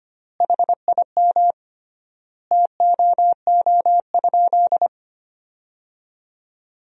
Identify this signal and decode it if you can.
Morse code, heard as a keyed tone, about 25 wpm, 700 Hz. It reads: HIM TOO?
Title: 3x Word-Spacing - Sentences from Top 500 Words - 25wpm